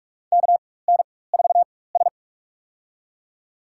Morse code, heard as a keyed tone, about 46 wpm, spaced as for 21 wpm, 700 Hz. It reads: KN4S
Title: USA Call Signs - 4 Characters - 4x Character Spacing - 45wpm